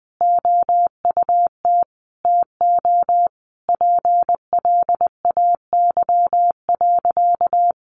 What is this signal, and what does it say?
Morse code, heard as a keyed tone, about 20 wpm, 700 Hz. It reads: OUT TO PLAY.
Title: Easy Phrases 20wpm